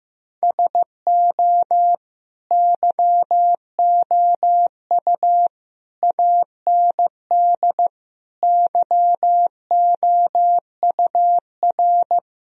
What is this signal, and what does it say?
Morse code, heard as a keyed tone, about 15 wpm, 700 Hz. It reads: SO YOU AND YOUR
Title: Sentences from Top 1000 Words - Review Part 4 - 15wpm